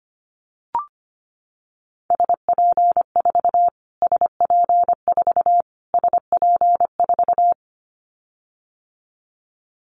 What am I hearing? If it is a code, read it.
Morse code, heard as a keyed tone, about 25 wpm, 700 Hz. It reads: SP4 SP4 SP4
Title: Callsign Prefixes for Top 25 DXCC - Repeated 3x - 25wpm